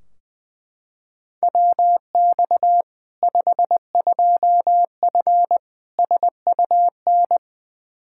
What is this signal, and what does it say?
Morse code, heard as a keyed tone, about 21 wpm, 700 Hz. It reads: WX 52F SUN
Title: Practice QSO - Exchange 2 - Info - 20wpm